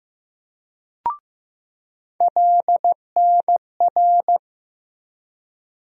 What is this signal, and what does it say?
Morse code, heard as a keyed tone, about 15 wpm, 700 Hz. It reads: LNR
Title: USA Call Sign Postfixes - 3 Letters 15wpm